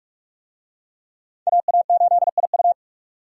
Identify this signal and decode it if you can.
Morse code, heard as a keyed tone, about 46 wpm, 700 Hz. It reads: AA8IU
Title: USA Call Signs - 5 Characters - 45wpm